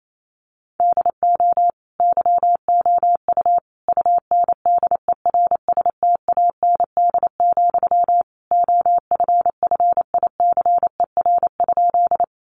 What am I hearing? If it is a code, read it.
Morse code, heard as a keyed tone, about 28 wpm, 700 Hz. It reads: DO YOU UNDERSTAND, OFFICER?
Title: Sentences from Top 1200 Words 28wpm